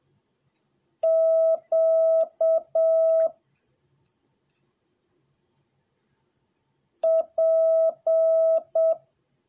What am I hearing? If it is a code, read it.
Morse code, heard as a keyed tone, about 7 wpm, 650 Hz. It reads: Q P